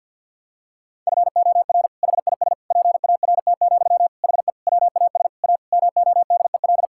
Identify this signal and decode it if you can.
Morse code, heard as a keyed tone, about 50 wpm, 690 Hz. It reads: FOR HIS PART, HE WAS A MODEL
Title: Sentences from Top 1000 Words - Review Part 5 - 50wpm